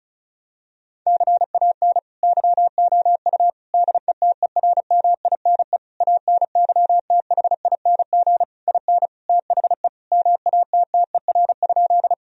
Morse code, heard as a keyed tone, about 36 wpm, 700 Hz. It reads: CAN YOU DETERMINE ANYTHING IN THE MATTER?
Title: Sentences from Top 1000 Words - Review Part 4 - 35wpm